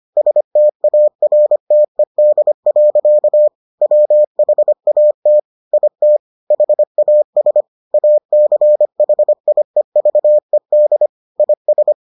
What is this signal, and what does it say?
Morse code, heard as a keyed tone, about 25 wpm, 600 Hz. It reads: STARTED. WHAT IT HAS ACHIEVED IS